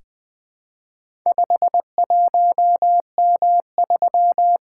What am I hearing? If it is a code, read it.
Morse code, heard as a keyed tone, about 20 wpm, 700 Hz. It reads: 51M3
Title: Four Letter-Number Combinations - 20wpm